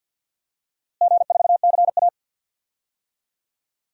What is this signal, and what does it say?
Morse code, heard as a keyed tone, about 52 wpm, 700 Hz. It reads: G4CA
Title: DXCC-100 Call Signs - 4 Characters - Repeated 3x - 50wpm